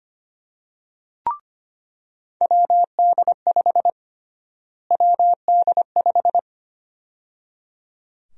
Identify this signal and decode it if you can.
Morse code, heard as a keyed tone, about 25 wpm, 700 Hz. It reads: WD5 WD5
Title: USA Call Sign Prefixes - Repeated 2x - 25wpm